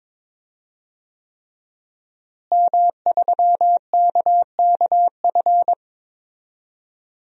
Morse code, heard as a keyed tone, about 22 wpm, 700 Hz. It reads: M3KKF
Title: DXCC-100 Call Signs - 5 Characters - Repeated 3x - 22wpm